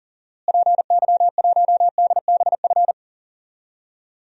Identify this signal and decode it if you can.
Morse code, heard as a keyed tone, about 40 wpm, 700 Hz. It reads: PY1DBF
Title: DXCC-100 Call Signs - 6 Characters - Repeated 2x - 40wpm